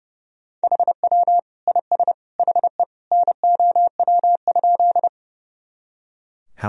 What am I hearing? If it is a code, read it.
Morse code, heard as a keyed tone, about 30 wpm, 700 Hz. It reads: HW IS HE NOW?
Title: Sentences from Top 100 Words and Abbreviations 30wpm